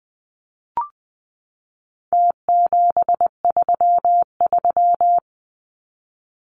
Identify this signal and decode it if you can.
Morse code, heard as a keyed tone, about 20 wpm, 700 Hz. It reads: T733
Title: Numbers - Contest Style - 4-Digit 20wpm